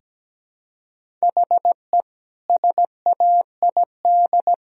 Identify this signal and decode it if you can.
Morse code, heard as a keyed tone, about 17 wpm, 700 Hz. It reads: HE SAID
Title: Sentences from Top 1000 Words - Review Part 2 - 17wpm